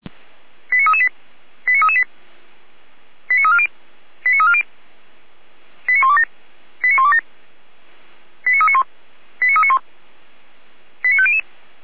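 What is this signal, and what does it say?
Five-tone selective call (ZVEI), heard as quick runs of five tones, tones of about 70 ms. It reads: 89338 89338 89340 89340 89127 89127 89391 89391 89500
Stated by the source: ZVEI five-tone selective call 89305 received by SDR, archived by Feuerwehr Orlamünde.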